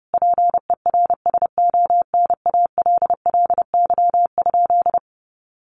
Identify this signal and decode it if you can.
Morse code, heard as a keyed tone, about 30 wpm, 700 Hz. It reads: PERSONALLY?
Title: Sentences from Top 1400 Words 30wpm